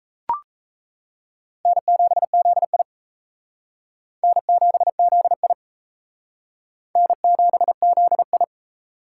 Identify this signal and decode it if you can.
Morse code, heard as a keyed tone, about 34 wpm, 700 Hz. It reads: N7ZI N7ZI N7ZI
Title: Speed Racing - USA 1x2 Call Signs 28wpm